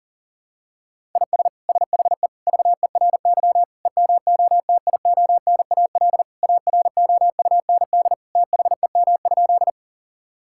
Morse code, heard as a keyed tone, about 40 wpm, 700 Hz. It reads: IS SHE VERY EMOTIONAL AROUND THEM?
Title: Sentences from Top 1000 Words - Review Part 5 - 40wpm